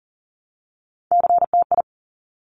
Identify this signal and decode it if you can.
Morse code, heard as a keyed tone, about 38 wpm, 690 Hz. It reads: CTI
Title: Top Three Letter Combinations in Words 40wpm